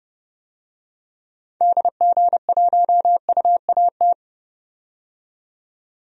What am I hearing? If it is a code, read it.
Morse code, heard as a keyed tone, about 32 wpm, 700 Hz. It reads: DG1UAT
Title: DXCC-100 Call Signs - 6 Characters - 30wpm